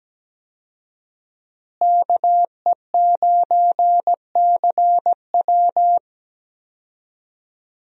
Morse code, heard as a keyed tone, about 17 wpm, 700 Hz. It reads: KE9CW
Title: USA Call Signs - 5 Characters - Repeated 3x - 17wpm